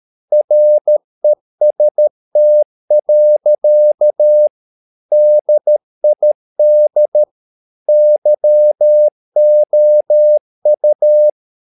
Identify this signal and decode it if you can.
Morse code, heard as a keyed tone, about 13 wpm, 600 Hz. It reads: REST. DID YOU
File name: mp3 files / Lexiphiles - 13wpm